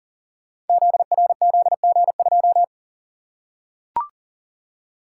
Morse code, heard as a keyed tone, about 42 wpm, 690 Hz. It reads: ZRZG2
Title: Code Groups - 5 Letter:Numbers - 40wpm